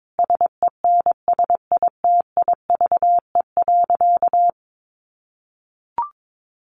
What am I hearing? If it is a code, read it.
Morse code, heard as a keyed tone, about 22 wpm, 700 Hz. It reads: SENSITIVE.
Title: Sentences from Top 1400 Words 22wpm